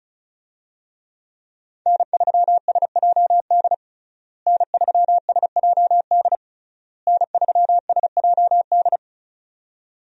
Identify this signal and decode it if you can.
Morse code, heard as a keyed tone, about 36 wpm, 700 Hz. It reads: N3SJD N3SJD N3SJD
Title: USA Call Signs - 5 Characters - Repeated 3x - 35wpm